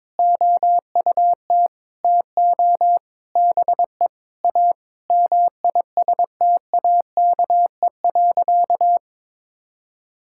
Morse code, heard as a keyed tone, about 22 wpm, 700 Hz. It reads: OUT TO BE A MISTAKE.
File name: practice / 22wpm / Sentences with 4-Word Phrases - 22wpm